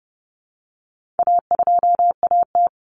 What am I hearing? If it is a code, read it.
Morse code, heard as a keyed tone, about 30 wpm, 700 Hz. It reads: A2AT
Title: Common Cut Numbers - 4-Digit 30wpm